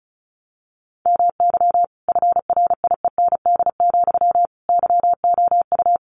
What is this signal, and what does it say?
Morse code, heard as a keyed tone, about 34 wpm, 700 Hz. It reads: MY FRIEND, YOU
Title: Sentences from Top 1300 Words 35wpm